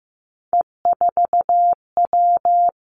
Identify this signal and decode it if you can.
Morse code, heard as a keyed tone, about 15 wpm, 700 Hz. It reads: E4W
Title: Three Challenging Letter-Number Combinations 15wpm